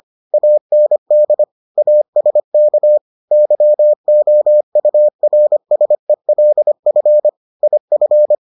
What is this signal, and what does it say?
Morse code, heard as a keyed tone, about 25 wpm, 600 Hz. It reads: AND ASK YOURSELF IF